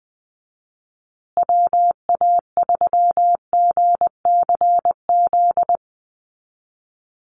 Morse code, heard as a keyed tone, about 20 wpm, 700 Hz. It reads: WA3GCZ
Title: USA Call Signs Repeated 3x 20wpm